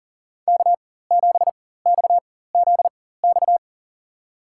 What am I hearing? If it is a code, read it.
Morse code, heard as a keyed tone, about 40 wpm, spaced as for 19 wpm, 700 Hz. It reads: K7XZX
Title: USA Call Signs - 4x Character Spacing - 40wpm